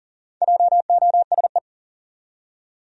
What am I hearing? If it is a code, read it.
Morse code, heard as a keyed tone, about 39 wpm, 700 Hz. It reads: JOSE
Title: USA Common Names - Max 4 Letters - 40wpm